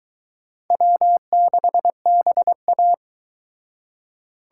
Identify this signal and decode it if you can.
Morse code, heard as a keyed tone, about 23 wpm, 700 Hz. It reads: W6BA